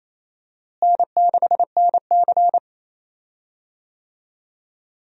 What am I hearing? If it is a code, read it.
Morse code, heard as a keyed tone, about 28 wpm, 700 Hz. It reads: N6NC
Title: USA Call Signs - 4 Characters - Repeated 3x - 28wpm